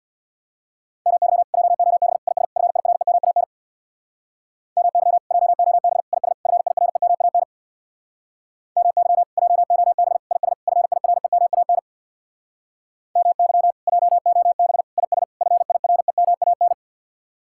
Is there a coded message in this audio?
Yes: Morse code from 1.1 to 16.7 s, heard as a keyed tone, about 66 wpm, 700 Hz. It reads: MY JOB IS FIREMAN MY JOB IS FIREMAN MY JOB IS FIREMAN MY JOB IS FIREMAN